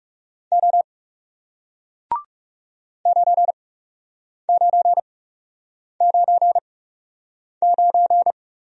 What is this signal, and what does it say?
Morse code, heard as a keyed tone, about 38 wpm, spaced as for 8 wpm, 700 Hz. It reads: O 9999